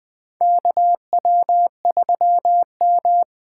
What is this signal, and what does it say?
Morse code, heard as a keyed tone, about 20 wpm, 700 Hz. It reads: KW3M
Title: USA Call Signs - 2x Character Spacing - 20wpm